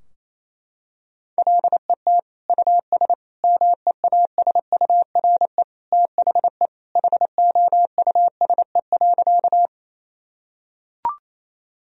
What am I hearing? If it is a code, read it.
Morse code, heard as a keyed tone, about 28 wpm, 700 Hz. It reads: LET US MEASURE THE HOUSE.
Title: Sentences from Top 1000 Words - Review Part 2 - 28wpm